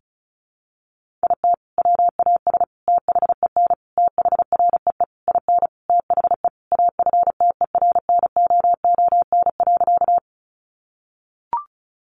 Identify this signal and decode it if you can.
Morse code, heard as a keyed tone, about 34 wpm, 700 Hz. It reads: IT WAS THEN THREE IN THE AFTERNOON.
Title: Sentences from Top 1500 Words 35wpm